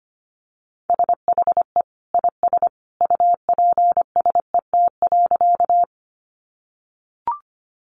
Morse code, heard as a keyed tone, about 25 wpm, 700 Hz. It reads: SHE IS UPSET.
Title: Sentences from Top 800 Words 25wpm